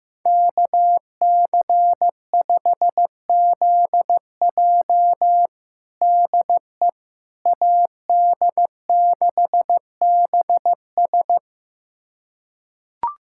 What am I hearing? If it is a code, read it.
Morse code, heard as a keyed tone, about 15 wpm, 700 Hz. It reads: KC5ZJ DE AD6BS
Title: USA Callsign DE Callsign - 5 Characters - 15wpm